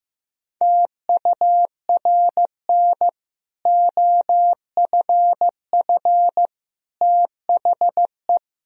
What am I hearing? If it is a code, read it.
Morse code, heard as a keyed tone, about 15 wpm, 700 Hz. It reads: TURN OFF THE
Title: Sentences from Top 1000 Words - Review Part 3 - 15wpm